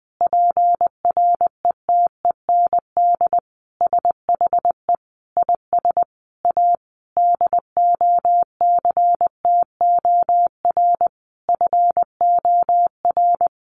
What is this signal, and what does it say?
Morse code, heard as a keyed tone, about 20 wpm, 700 Hz. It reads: PRETEND SHE IS A DOCTOR FOR
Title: Additional Easy Phrases 20wpm